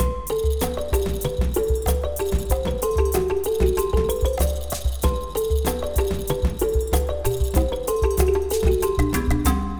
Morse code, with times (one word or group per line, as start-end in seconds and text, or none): none